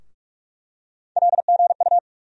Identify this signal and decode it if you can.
Morse code, heard as a keyed tone, about 46 wpm, 700 Hz. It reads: LGU
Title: USA Call Sign Postfixes - 3 Letters 45wpm